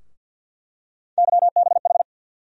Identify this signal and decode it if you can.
Morse code, heard as a keyed tone, about 48 wpm, 700 Hz. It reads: YBH
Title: USA Call Sign Postfixes - 3 Letters 50wpm